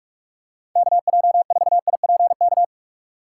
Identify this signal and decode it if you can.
Morse code, heard as a keyed tone, about 46 wpm, 700 Hz. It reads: KJ4IPK
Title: USA Call Signs - 6x Character Spacing - 45wpm